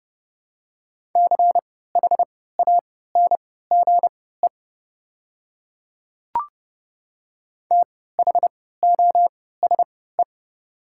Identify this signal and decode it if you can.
Morse code, heard as a keyed tone, about 30 wpm, spaced as for 17 wpm, 690 Hz. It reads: CHANGE THOSE
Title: 3x Character Spacing - Top 300 Words - 30wpm